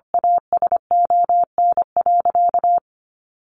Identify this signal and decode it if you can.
Morse code, heard as a keyed tone, about 25 wpm, 700 Hz. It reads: ASON.
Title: Sentences from Top 1400 Words 25wpm